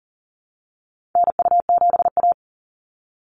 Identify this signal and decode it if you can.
Morse code, heard as a keyed tone, about 40 wpm, 700 Hz. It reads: NU7A
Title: NA SOTA Chaser Callsigns 40wpm